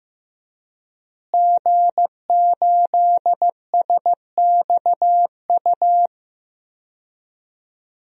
Morse code, heard as a keyed tone, about 15 wpm, 700 Hz. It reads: G8SXU
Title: DXCC-100 Call Signs - 5 Characters - Repeated 2x - 15wpm